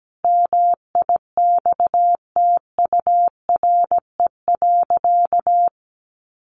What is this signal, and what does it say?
Morse code, heard as a keyed tone, about 17 wpm, 700 Hz. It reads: MIXTURE.
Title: Sentences from Top 1500 Words 17wpm